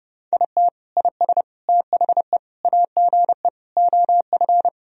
Morse code, heard as a keyed tone, about 30 wpm, 700 Hz. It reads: IT IS THE AGE OF
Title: Sentences from Top 1000 Words - Review Part 3 - 30wpm